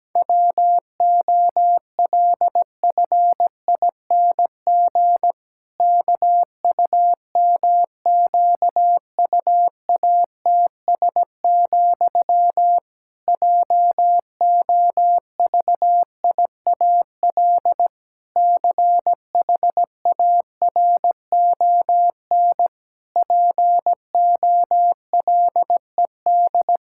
Morse code, heard as a keyed tone, about 17 wpm, 700 Hz. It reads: WOLFING KUMQUATS, JOVIAL CHARON POLED
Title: Pangram Challenge v2 - 17wpm